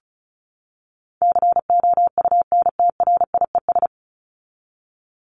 Morse code, heard as a keyed tone, about 34 wpm, 700 Hz. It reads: COUNTRIES
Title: ES:S Suffix - Top 100 Words with Base Words 35wpm